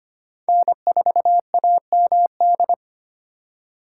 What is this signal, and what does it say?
Morse code, heard as a keyed tone, about 31 wpm, 700 Hz. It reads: N4AMD
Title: Speed Racing - USA Call Signs 25wpm